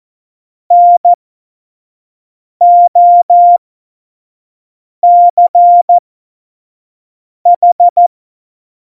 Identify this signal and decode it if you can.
Morse code, heard as a keyed tone, about 14 wpm, spaced as for 5 wpm, 700 Hz. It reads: NOCH
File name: Morsekurs / Witze / Witz 28. November 2017, 25 F70